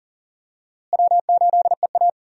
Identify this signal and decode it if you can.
Morse code, heard as a keyed tone, about 40 wpm, 700 Hz. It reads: W8EA
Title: USA Call Signs - 4 Characters - Repeated 3x - 40wpm